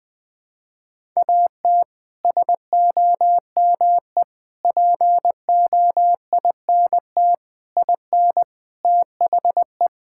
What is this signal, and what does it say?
Morse code, heard as a keyed tone, about 20 wpm, 700 Hz. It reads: AT SOME POINT IN THE
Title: Sets of 6 Words - 20wpm